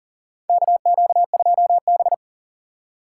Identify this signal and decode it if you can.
Morse code, heard as a keyed tone, about 40 wpm, 700 Hz. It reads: KQ2B
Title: USA Call Signs - 4 Characters - 4x Character Spacing - 40wpm